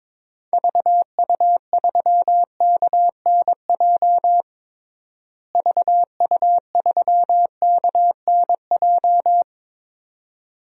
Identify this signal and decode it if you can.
Morse code, heard as a keyed tone, about 22 wpm, 700 Hz. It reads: VU3KNJ VU3KNJ
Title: DXCC-100 Call Signs - 6 Characters - Repeated 2x - 22wpm